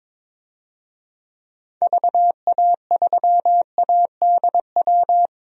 Morse code, heard as a keyed tone, about 22 wpm, 700 Hz. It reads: VA3ADW
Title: DXCC-100 Call Signs - 6 Characters - 22wpm